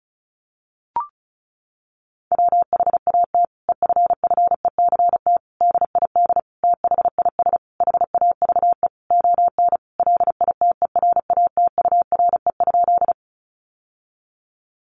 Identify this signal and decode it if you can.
Morse code, heard as a keyed tone, about 34 wpm, 700 Hz. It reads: WHAT EFFECT DID THIS HAVE ON LITERATURE?
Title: Sentences from Top 1000 Words 35wpm